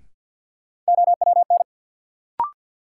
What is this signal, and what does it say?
Morse code, heard as a keyed tone, about 53 wpm, 700 Hz. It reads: OWN
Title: Sets of 2 Words - New Word Drill - 50wpm